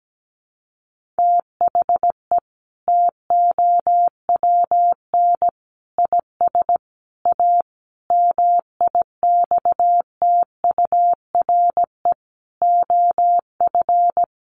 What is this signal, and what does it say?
Morse code, heard as a keyed tone, about 17 wpm, 700 Hz. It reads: THE TOWN IS A MIXTURE OF
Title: Sentences from Top 1500 Words 17wpm